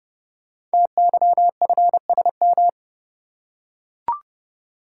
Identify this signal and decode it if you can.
Morse code, heard as a keyed tone, about 29 wpm, 700 Hz. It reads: TYFSM
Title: Code Groups - 5 Letters - 30wpm